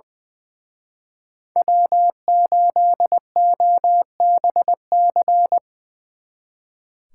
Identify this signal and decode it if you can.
Morse code, heard as a keyed tone, about 24 wpm, 700 Hz. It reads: W8OBC
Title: Speed Racing - USA Call Signs 20wpm